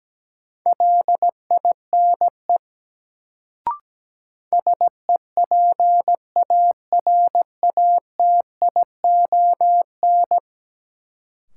Morse code, heard as a keyed tone, about 17 wpm, 700 Hz. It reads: LINE SEPARATION
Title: Sets of 5 Words - New Word Drill - 17wpm